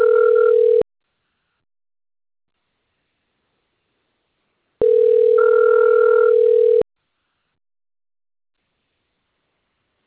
Ringback tone (North America), 440 + 480 Hz.